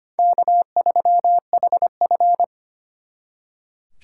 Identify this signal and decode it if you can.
Morse code, heard as a keyed tone, about 33 wpm, 700 Hz. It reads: K3HF